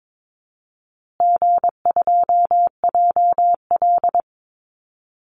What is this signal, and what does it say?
Morse code, heard as a keyed tone, about 22 wpm, 700 Hz. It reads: G2JL
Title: Easy Call Signs Repeated 3x - 22wpm